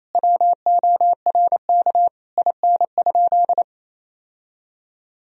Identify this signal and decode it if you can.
Morse code, heard as a keyed tone, about 27 wpm, 700 Hz. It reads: WORK IN?
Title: Sentences from Top 1000 Words - Review Part 4 - 28wpm